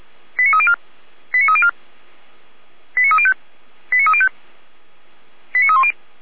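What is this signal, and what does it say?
Five-tone selective call (ZVEI), heard as quick runs of five tones, tones of about 70 ms. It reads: 89304 89304 89305 89305 89310
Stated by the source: ZVEI five-tone selective call received by SDR, hosted by feuerwehr-orlamuende (Orlamünde fire brigade).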